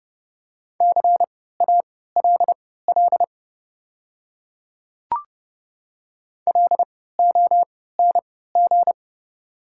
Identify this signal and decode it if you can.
Morse code, heard as a keyed tone, about 32 wpm, spaced as for 17 wpm, 700 Hz. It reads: CALL LONG
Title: 3x Character Spacing - Top 100 Words - 30wpm